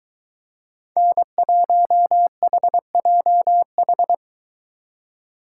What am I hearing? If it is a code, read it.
Morse code, heard as a keyed tone, about 23 wpm, 700 Hz. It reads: N1HJH